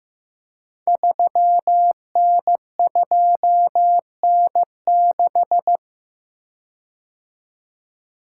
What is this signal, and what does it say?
Morse code, heard as a keyed tone, about 15 wpm, 700 Hz. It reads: 3N2N6
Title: Code Groups - 5 Letter:Numbers - 15wpm